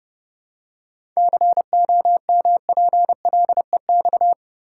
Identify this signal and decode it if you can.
Morse code, heard as a keyed tone, about 30 wpm, 700 Hz. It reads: COMPLEX
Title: Seven Letter Words - 30wpm